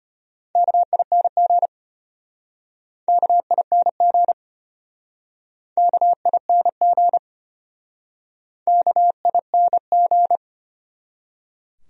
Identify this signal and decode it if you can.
Morse code, heard as a keyed tone, about 33 wpm, 700 Hz. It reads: KING KING KING KING